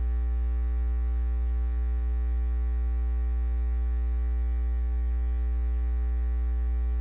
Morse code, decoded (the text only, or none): none